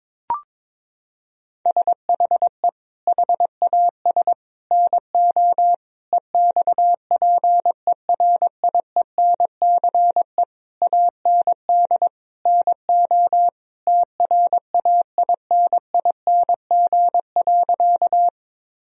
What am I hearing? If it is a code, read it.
Morse code, heard as a keyed tone, about 22 wpm, 700 Hz. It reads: SHE HAS NO EXPERIENCE AND NO TRAINING.
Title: Sentences from Top 1000 Words - Review Part 4 - 22wpm